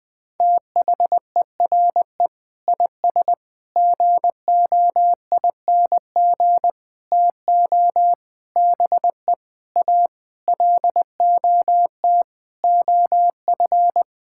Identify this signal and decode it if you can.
Morse code, heard as a keyed tone, about 21 wpm, 700 Hz. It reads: THERE IS GOING TO BE A LOT OF
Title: Sentences with 7-Word Phrases - 20wpm